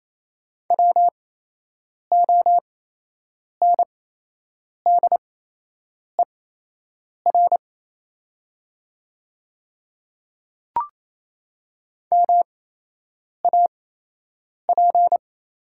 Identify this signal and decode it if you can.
Morse code, heard as a keyed tone, about 28 wpm, spaced as for 8 wpm, 700 Hz. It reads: WONDER MAP
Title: 8x Character Spacing - Top 500 Words - 28wpm